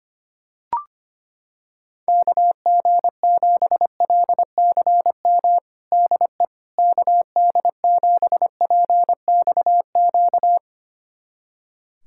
Morse code, heard as a keyed tone, about 25 wpm, 700 Hz. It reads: KG7LCM DE KD7PXQ
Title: USA Callsign DE Callsign - 6 Characters - 25wpm